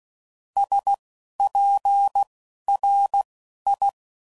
Morse code, heard as a keyed tone, about 16 wpm, spaced as for 11 wpm, 800 Hz. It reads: SPRI
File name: CW Code Groups @10WPM / QSOs / QSO WA7QOX